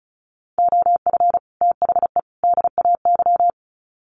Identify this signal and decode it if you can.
Morse code, heard as a keyed tone, about 34 wpm, 700 Hz. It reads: OF THE DAY
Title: Sets of 3 Words 35wpm